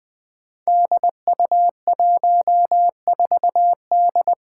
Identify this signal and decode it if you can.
Morse code, heard as a keyed tone, about 21 wpm, 700 Hz. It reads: DU14D
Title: Five Challenging Letter-Number Combinations 20wpm